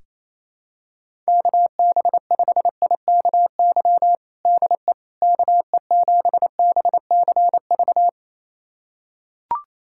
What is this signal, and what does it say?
Morse code, heard as a keyed tone, about 28 wpm, 700 Hz. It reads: KB5IKY DE KE7BCV
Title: USA Callsign DE Callsign - 6 Characters - 28wpm